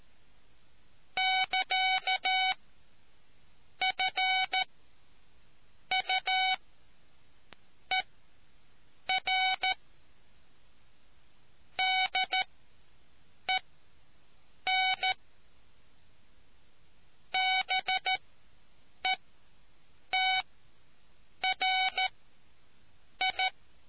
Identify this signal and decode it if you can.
Morse code, heard as a keyed tone, about 13 wpm, spaced as for 6 wpm, 750 Hz. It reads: <KA> FUER DEN BETRI